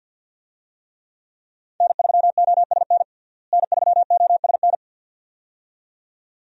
Morse code, heard as a keyed tone, about 50 wpm, 700 Hz. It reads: N3OSN N3OSN
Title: USA Call Signs - 5 Characters - Repeated 2x - 50wpm